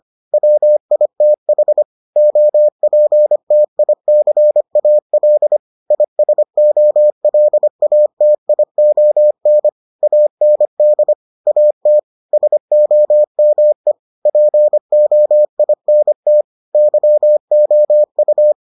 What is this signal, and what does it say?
Morse code, heard as a keyed tone, about 25 wpm, 600 Hz. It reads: WITH OPTICAL ISOLATION AND AT SOME POINT YOU